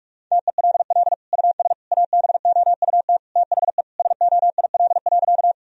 Morse code, heard as a keyed tone, about 46 wpm, 700 Hz. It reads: TELL US ABOUT THE SOIL.
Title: Sentences from Top 1000 Words - Review Part 4 - 45wpm